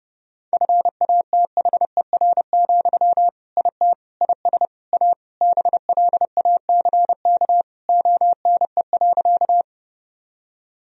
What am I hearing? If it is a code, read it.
Morse code, heard as a keyed tone, about 30 wpm, 700 Hz. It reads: FATHER, IT IS A BLACK ONE.
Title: Sentences from Top 1000 Words - Review Part 2 - 30wpm